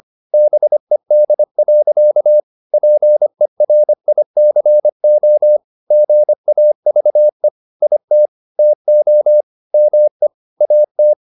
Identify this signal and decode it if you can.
Morse code, heard as a keyed tone, about 25 wpm, 600 Hz. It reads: BED. PERICO GAVE IT TO ME AT
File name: the old man and the sea part 1 25wpm